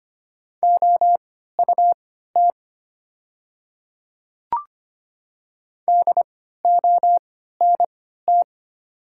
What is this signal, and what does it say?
Morse code, heard as a keyed tone, about 25 wpm, spaced as for 14 wpm, 700 Hz. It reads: OUT DONT
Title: 3x Character Spacing - Top 300 Words - 25wpm